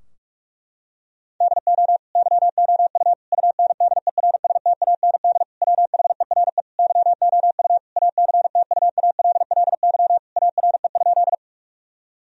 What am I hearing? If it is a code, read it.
Morse code, heard as a keyed tone, about 46 wpm, 700 Hz. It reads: DO YOU UNDERSTAND WHERE YOU ACTUALLY ARE?
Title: Sentences from Top 1000 Words - Review Part 4 - 45wpm